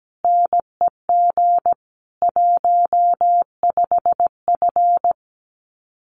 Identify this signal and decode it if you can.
Morse code, heard as a keyed tone, about 17 wpm, 700 Hz. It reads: NEG 15F
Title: Common QSO Phrases Encore 17wpm